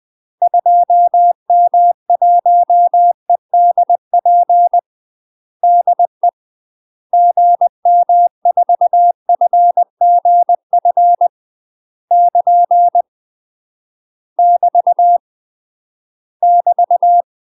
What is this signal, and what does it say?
Morse code, heard as a keyed tone, about 20 wpm, 700 Hz. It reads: 2M1EDP DE GM4FGF <KN> <BT> <BT>
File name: UK MASTER QSOs TRAINING / 20WPM / UK MASTER QSOs 0070